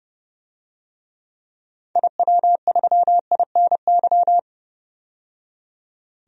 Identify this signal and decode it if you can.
Morse code, heard as a keyed tone, about 32 wpm, 700 Hz. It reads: IW3INY
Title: DXCC-100 Call Signs - 6 Characters - 30wpm